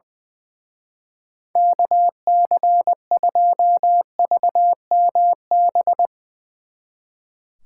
Morse code, heard as a keyed tone, about 22 wpm, 700 Hz. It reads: KC2VMB
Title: Speed Racing - USA 2x3 Call Signs 20wpm